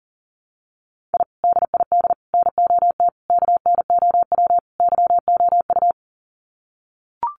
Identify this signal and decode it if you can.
Morse code, heard as a keyed tone, about 40 wpm, 700 Hz. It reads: I DID NOT KNOW YOU
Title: Sets of 5 Words Encore2 40wpm